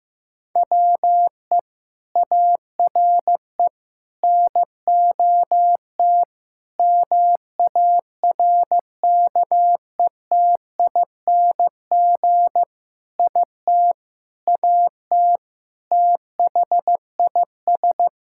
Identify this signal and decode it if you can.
Morse code, heard as a keyed tone, about 15 wpm, 700 Hz. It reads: WE ARE NOT MARKETING IT AT THIS
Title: Sentences from Top 1000 Words - Review Part 5 - 15wpm